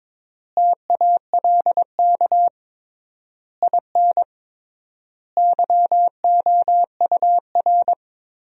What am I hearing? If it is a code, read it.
Morse code, heard as a keyed tone, about 22 wpm, 700 Hz. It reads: TALK IN YOUR
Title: 3x Word-Spacing - Sentences from Top 500 Words - 22wpm